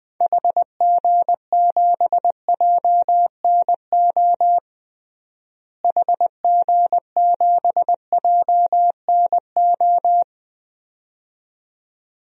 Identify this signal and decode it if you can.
Morse code, heard as a keyed tone, about 20 wpm, 700 Hz. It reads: HG7JNO HG7JNO
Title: DXCC-100 Call Signs - 6 Characters - Repeated 2x - 20wpm